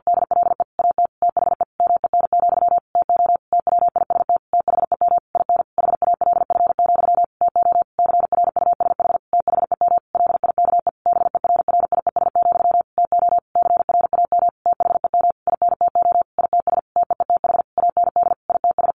Morse code, heard as a keyed tone, about 50 wpm, 700 Hz. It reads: DLE AT THE MEN, TO TWIST THEM IN HALF, TO CRUSH THEM LIKE BERRIES, TO CRAM THEM INTO ITS TEETH AND ITS
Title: A Sound Of Thunder 50wpm